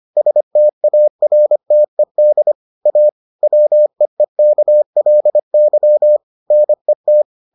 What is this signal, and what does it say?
Morse code, heard as a keyed tone, about 25 wpm, 600 Hz. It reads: STARTED A WEEKLY NET